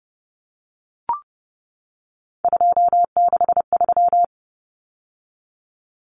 Morse code, heard as a keyed tone, about 30 wpm, 700 Hz. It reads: 263
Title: Numbers - 3-Digit 30wpm